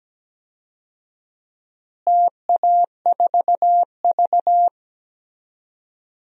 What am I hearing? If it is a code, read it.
Morse code, heard as a keyed tone, about 17 wpm, 700 Hz. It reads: TA4V
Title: DXCC-100 Call Signs - 4 Characters - Repeated 3x - 17wpm